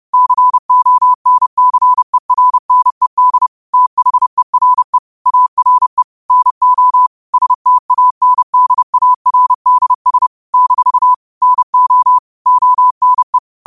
Morse code, heard as a keyed tone, about 30 wpm, 1000 Hz. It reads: CONCERNED THERE ARE NO STANDARDS - NO ONE
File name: qotdfiles / 30wpm